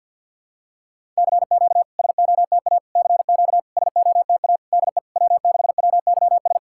Morse code, heard as a keyed tone, about 52 wpm, 700 Hz. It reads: CQ SOTA CQ SOTA DE W6WYS
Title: SOTA QSO Practice 50wpm